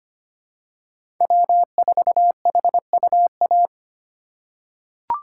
Morse code, heard as a keyed tone, about 25 wpm, 700 Hz. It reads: W4HUA
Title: USA Call Signs - 3x Character Spacing - 25wpm